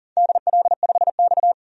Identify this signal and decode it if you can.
Morse code, heard as a keyed tone, about 37 wpm, 700 Hz. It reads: DL5X
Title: DXCC-100 Call Signs - 4 Characters - 40wpm